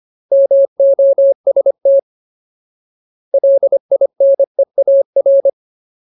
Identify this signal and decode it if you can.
Morse code, heard as a keyed tone, about 25 wpm, 550 Hz. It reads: MOST LINEAR
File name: Ultimate Word List / 25WPM Ultimate